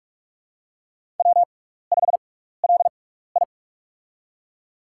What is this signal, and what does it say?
Morse code, heard as a keyed tone, about 44 wpm, spaced as for 15 wpm, 700 Hz. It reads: W5LI
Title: USA Call Signs - 4 Characters - 6x Character Spacing - 45wpm